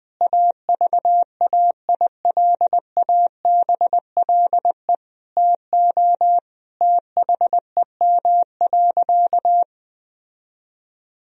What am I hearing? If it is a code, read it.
Morse code, heard as a keyed tone, about 21 wpm, 700 Hz. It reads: AVAILABLE TO THEM.
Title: Sentences from Top 1000 Words - Review Part 3 - 20wpm